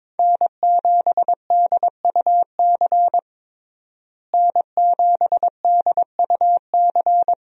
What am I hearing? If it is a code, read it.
Morse code, heard as a keyed tone, about 22 wpm, 700 Hz. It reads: N7DUC N7DUC
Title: USA Call Signs - 5 Characters - Repeated 3x - 22wpm